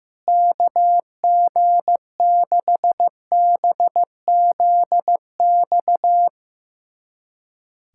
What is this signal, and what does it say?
Morse code, heard as a keyed tone, about 15 wpm, 700 Hz. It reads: KG6BZX
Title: USA Callsign DE Callsign - 6 Characters - 15wpm